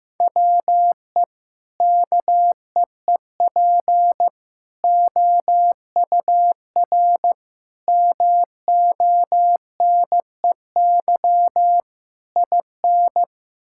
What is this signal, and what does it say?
Morse code, heard as a keyed tone, about 15 wpm, 700 Hz. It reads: WE KEEP OUR MONEY IN
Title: Sentences from Top 1000 Words - Review Part 6 - 15wpm